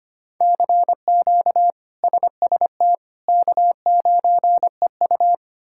Morse code, heard as a keyed tone, about 25 wpm, 700 Hz. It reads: CQ SST K9EU
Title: K1USN SST QSO Practice - 25wpm